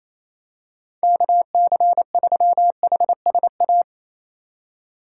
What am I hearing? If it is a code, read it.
Morse code, heard as a keyed tone, about 28 wpm, 700 Hz. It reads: KC3HSA
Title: USA Call Signs - 6 Characters - 3x Character Spacing - 28wpm